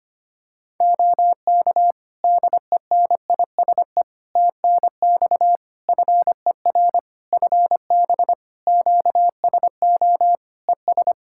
Morse code, heard as a keyed tone, about 25 wpm, 700 Hz. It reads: OK DENISE TNX FER FB QSO ES
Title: Practice QSO - Exchange 4 - Ending - 25wpm